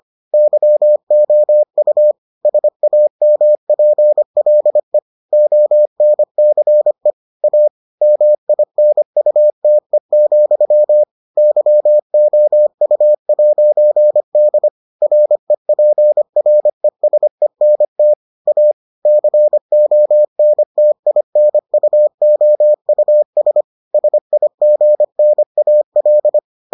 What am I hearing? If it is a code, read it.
Morse code, heard as a keyed tone, about 25 wpm, 600 Hz. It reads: YOU SAMPLE ONCE A MINUTE, YOU'D REPRESENT A CONTINUOUS SIGNAL